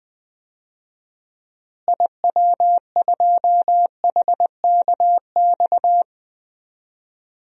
Morse code, heard as a keyed tone, about 20 wpm, 700 Hz. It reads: IW2HKX
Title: DXCC-100 Call Signs - 6 Characters - Repeated 3x - 20wpm